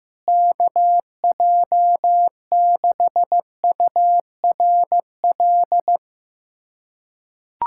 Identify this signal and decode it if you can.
Morse code, heard as a keyed tone, about 16 wpm, 700 Hz. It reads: KJ6URL
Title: USA Callsign DE Callsign - 6 Characters - 15wpm